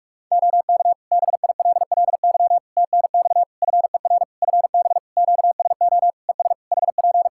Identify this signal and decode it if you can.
Morse code, heard as a keyed tone, about 41 wpm, 700 Hz. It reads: OK BILLY TNX FER FB QSO ES HP
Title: Practice QSO - Exchange 4 - Ending - 45wpm